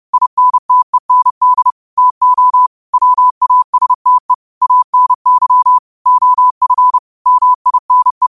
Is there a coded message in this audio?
Yes: Morse code from 0.1 to 8.3 s, heard as a keyed tone, about 30 wpm, 1000 Hz. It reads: INTEND TO WASTE ANY OF MINE